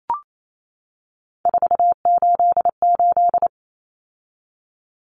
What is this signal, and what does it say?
Morse code, heard as a keyed tone, about 28 wpm, 700 Hz. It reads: 488
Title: Common Cut Numbers - 3-Digit 28wpm